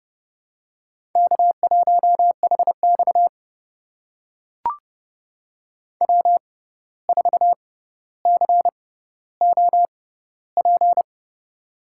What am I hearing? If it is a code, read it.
Morse code, heard as a keyed tone, about 30 wpm, spaced as for 14 wpm, 700 Hz. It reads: K1HX W4COP
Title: USA Call Signs - 6x Character Spacing - 30wpm